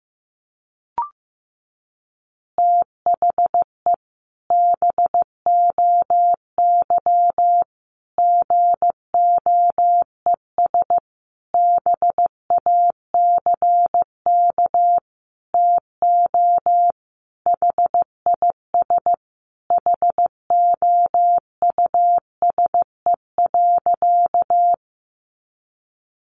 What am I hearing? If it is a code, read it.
Morse code, heard as a keyed tone, about 15 wpm, 700 Hz. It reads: THE BOY GOES BACK TO HIS HOUSE.
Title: Easy Phrases 15wpm